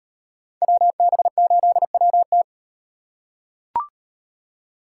Morse code, heard as a keyed tone, about 34 wpm, 700 Hz. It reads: WB8WT